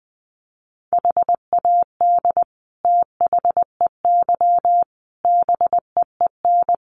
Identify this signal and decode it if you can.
Morse code, heard as a keyed tone, about 20 wpm, 700 Hz. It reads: HAD THEY BEEN
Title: Sentences from Top 400 Words 20wpm